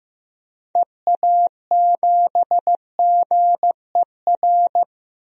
Morse code, heard as a keyed tone, about 15 wpm, 690 Hz. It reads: EA7GER
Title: DXCC-100 Call Signs - 6 Characters - 15wpm